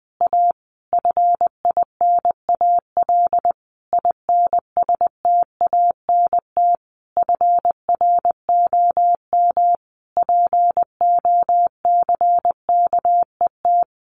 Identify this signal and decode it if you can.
Morse code, heard as a keyed tone, about 20 wpm, 700 Hz. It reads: A FINAL INSTANT FROM POCKET
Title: A Sound Of Thunder 20wpm